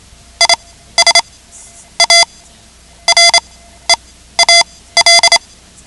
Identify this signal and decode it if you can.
Morse code, heard as a keyed tone, about 28 wpm, spaced as for 15 wpm, 2050 Hz. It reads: IS A REAL
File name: ham radio / CodeQuick / Code Quick 10